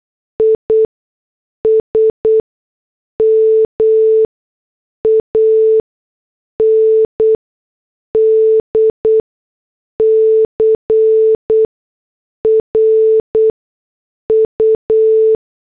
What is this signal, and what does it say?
Morse code, heard as a keyed tone, about 8 wpm, spaced as for 6 wpm, 440 Hz. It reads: ISM AND CRU